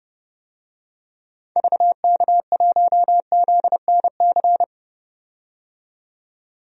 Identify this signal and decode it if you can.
Morse code, heard as a keyed tone, about 31 wpm, 700 Hz. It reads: VK1ZNC
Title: DXCC-100 Call Signs - 6 Characters - 30wpm